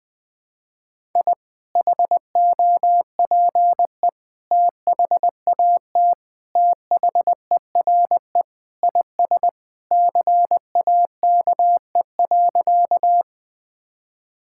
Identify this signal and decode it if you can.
Morse code, heard as a keyed tone, about 20 wpm, 700 Hz. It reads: I HOPE THAT THERE IS CAKE.
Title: Sentences with 2-Word Phrases - 20wpm